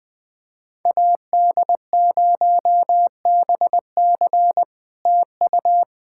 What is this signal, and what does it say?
Morse code, heard as a keyed tone, about 20 wpm, 700 Hz. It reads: AD0BC TU
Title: POTA QSO Practice - 20wpm